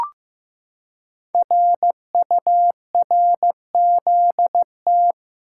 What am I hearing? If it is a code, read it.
Morse code, heard as a keyed tone, about 15 wpm, 700 Hz. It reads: RURZT
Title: Code Groups - 5 Letters - 15wpm